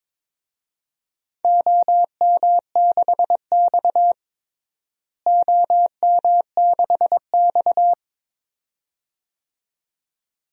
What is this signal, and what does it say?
Morse code, heard as a keyed tone, about 22 wpm, 700 Hz. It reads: OM6X OM6X
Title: DXCC-100 Call Signs - 4 Characters - Repeated 2x - 22wpm